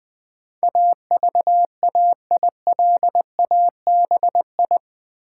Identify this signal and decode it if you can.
Morse code, heard as a keyed tone, about 20 wpm, 700 Hz. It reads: AVAILABI
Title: Build A Word - 4 Letters At A Time - 12-Letter Words - 20wpm